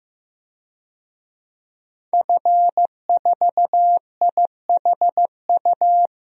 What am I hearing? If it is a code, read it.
Morse code, heard as a keyed tone, about 15 wpm, 700 Hz. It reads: F4IHU
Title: DXCC-100 Call Signs - 5 Characters - 15wpm